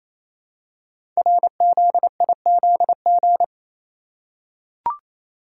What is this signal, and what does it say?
Morse code, heard as a keyed tone, about 28 wpm, 700 Hz. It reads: RZIZG
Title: Code Groups - 5 Letters - 28wpm